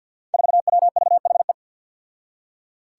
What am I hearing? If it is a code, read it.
Morse code, heard as a keyed tone, about 47 wpm, 700 Hz. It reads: 4WVHE
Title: Five Challenging Letter-Number Combinations 50wpm